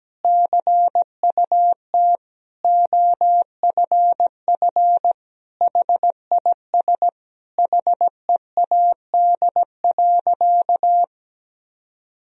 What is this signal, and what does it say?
Morse code, heard as a keyed tone, about 17 wpm, 700 Hz. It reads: CUT OFF HIS HEAD.
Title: Sentences from Top 1000 Words - Review Part 1 - 17wpm